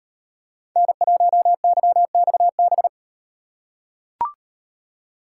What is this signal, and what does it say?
Morse code, heard as a keyed tone, about 38 wpm, 700 Hz. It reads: N1YXB